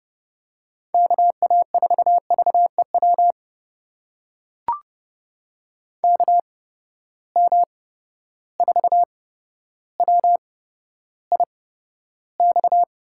Morse code, heard as a keyed tone, about 30 wpm, spaced as for 13 wpm, 700 Hz. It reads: KA4VEW KM4WIX
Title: USA Call Signs - 6 Characters - 8x Character Spacing - 30wpm